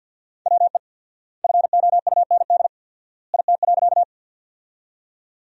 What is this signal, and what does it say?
Morse code, heard as a keyed tone, about 47 wpm, 700 Hz. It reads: WE FOUND IT.
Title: 4x Word-Spacing - Sentences from Top 300 Words - 50wpm